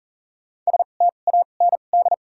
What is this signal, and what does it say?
Morse code, heard as a keyed tone, about 40 wpm, spaced as for 29 wpm, 700 Hz. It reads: STAND
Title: 2x Character Spacing - Top 500 Words - 40wpm